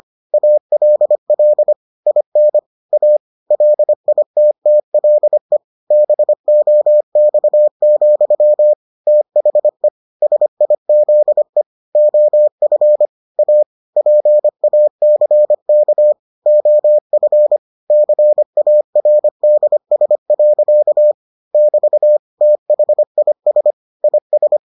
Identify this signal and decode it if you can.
Morse code, heard as a keyed tone, about 25 wpm, 600 Hz. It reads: ALL IN A LITTLE BOX, THE SIZE OF A PACK OF CARDS. <BT> THIS IS